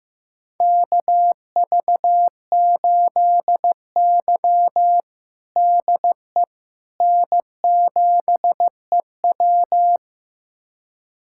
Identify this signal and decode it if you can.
Morse code, heard as a keyed tone, about 15 wpm, 700 Hz. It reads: KV8Y DE N7EW
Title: USA Callsign DE Callsign - 4 Characters - 15wpm